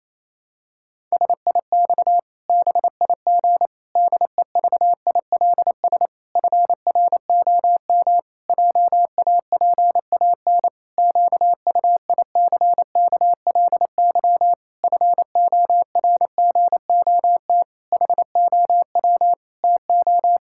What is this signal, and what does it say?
Morse code, heard as a keyed tone, about 28 wpm, 700 Hz. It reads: SIX BIG DEVILS FROM JAPAN QUICKLY FORGOT HOW TO
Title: Pangram Challenge v2 - 28wpm